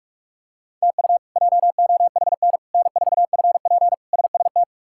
Morse code, heard as a keyed tone, about 46 wpm, 700 Hz. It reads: TU JOHN N4FP SST
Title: K1USN SST QSO Practice - 45wpm